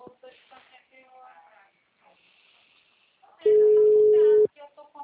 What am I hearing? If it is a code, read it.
Ringback tone (Europe, 425 Hz), with speech.